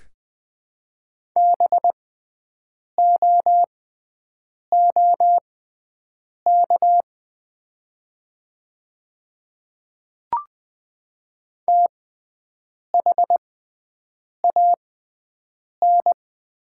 Morse code, heard as a keyed tone, about 20 wpm, spaced as for 7 wpm, 690 Hz. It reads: BOOK THAN
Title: 6x Character Spacing - Top 300 Words - 20wpm